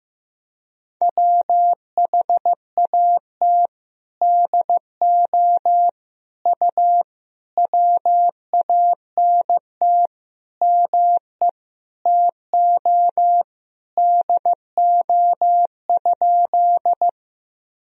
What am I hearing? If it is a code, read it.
Morse code, heard as a keyed tone, about 16 wpm, 700 Hz. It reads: WHAT DO U WANT ME TO DO?
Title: Sentences from Top 200 Words and Abbreviations 15wpm